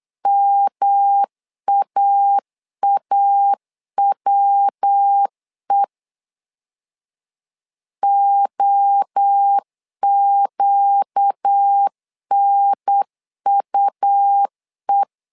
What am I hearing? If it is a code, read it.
Morse code, heard as a keyed tone, about 8 wpm, 800 Hz. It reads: MAAWE OQNUE
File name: 4 Q 8wpm